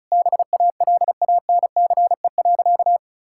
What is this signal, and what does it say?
Morse code, heard as a keyed tone, about 36 wpm, 700 Hz. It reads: BALANCE.
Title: Sentences from Top 1000 Words - Review Part 6 - 35wpm